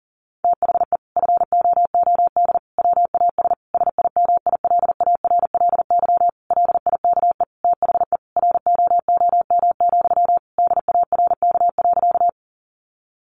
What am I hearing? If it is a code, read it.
Morse code, heard as a keyed tone, about 40 wpm, 700 Hz. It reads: THE FOOD WAS SIMILARLY LIKE THE ROOM, DARK.
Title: Sentences from Top 1400 Words 40wpm